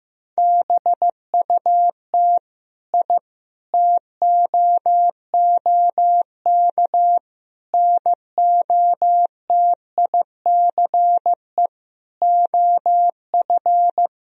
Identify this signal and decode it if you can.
Morse code, heard as a keyed tone, about 15 wpm, 700 Hz. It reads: BUT I TOOK NOTICE OF
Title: Sentences from Top 1000 Words - Review Part 2 - 15wpm